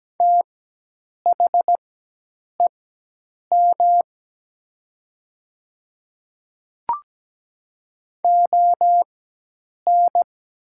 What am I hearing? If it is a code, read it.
Morse code, heard as a keyed tone, about 17 wpm, spaced as for 8 wpm, 700 Hz. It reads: THEM ON
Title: 4x Character Spacing - Top 100 Words - 17wpm